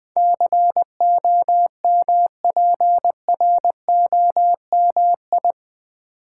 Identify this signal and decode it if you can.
Morse code, heard as a keyed tone, about 20 wpm, 700 Hz. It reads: COMPROMI
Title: Build A Word - 4 Letters At A Time - 10-Letter Words - 20wpm